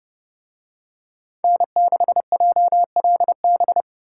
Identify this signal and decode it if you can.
Morse code, heard as a keyed tone, about 29 wpm, 700 Hz. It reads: N6JLB